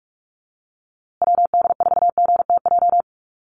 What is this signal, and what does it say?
Morse code, heard as a keyed tone, about 44 wpm, 700 Hz. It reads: WD4GTJ
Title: USA Call Signs Repeated 2x 45wpm